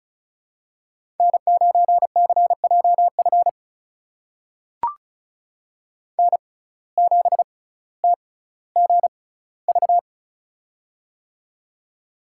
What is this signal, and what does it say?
Morse code, heard as a keyed tone, about 36 wpm, spaced as for 18 wpm, 700 Hz. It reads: N9CJF N7TGV
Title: USA Call Signs - 5 Characters - 6x Character Spacing - 35wpm